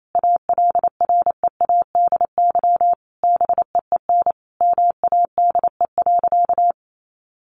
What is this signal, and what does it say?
Morse code, heard as a keyed tone, about 28 wpm, 700 Hz. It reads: ALREADY BEEN MADE.
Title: Sentences from Top 1400 Words 28wpm